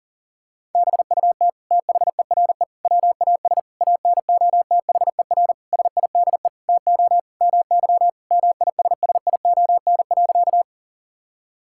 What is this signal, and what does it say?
Morse code, heard as a keyed tone, about 40 wpm, 700 Hz. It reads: BUT THERE WAS ANOTHER SIDE TO MY MISSION.
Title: Sentences from Top 1600 Words 40wpm